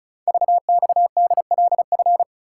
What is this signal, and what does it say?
Morse code, heard as a keyed tone, about 36 wpm, 700 Hz. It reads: VXDLF
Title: Code Groups - 5 Letters - 35wpm